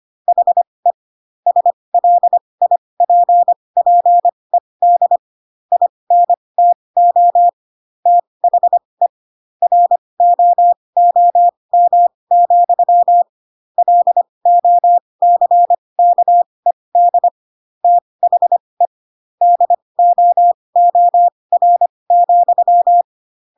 Morse code, heard as a keyed tone, about 25 wpm, spaced as for 20 wpm, 700 Hz. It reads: HE SLIPPED INTO THE ROOM, LOCKED THE DOOR,
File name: Pinocchio / Pinocchio ch4 15 25 ik1hge